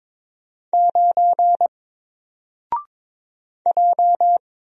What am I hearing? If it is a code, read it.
Morse code, heard as a keyed tone, about 22 wpm, 700 Hz. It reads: 9 J
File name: Single-Letter-Number-22wpm